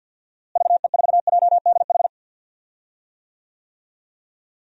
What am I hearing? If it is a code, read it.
Morse code, heard as a keyed tone, about 46 wpm, 700 Hz. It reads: VE4JDH
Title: DXCC-100 Call Signs - 6 Characters - Repeated 2x - 50wpm